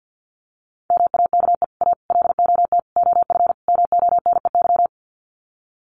Morse code, heard as a keyed tone, about 48 wpm, 700 Hz. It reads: MAKE A LOT OF MONEY
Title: Sets of 5 Words Encore2 50wpm